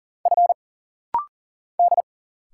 Morse code, heard as a keyed tone, about 40 wpm, 700 Hz. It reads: F D
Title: Single Letters - Rapid-Fire 40wpm